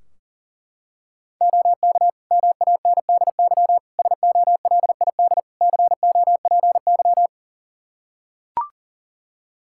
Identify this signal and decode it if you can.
Morse code, heard as a keyed tone, about 40 wpm, 700 Hz. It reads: OK MANDY SOLID COPY
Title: Practice QSO - Exchange 3 - More Info - 40wpm